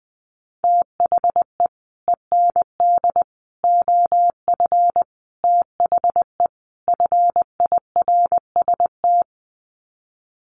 Morse code, heard as a keyed tone, about 20 wpm, 700 Hz. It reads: THE END OF THE FIRST
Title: Sets of 5 Words Encore 20wpm